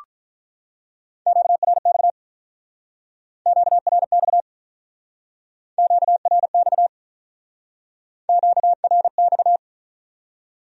Morse code, heard as a keyed tone, about 46 wpm, 700 Hz. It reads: QRX QRX QRX QRX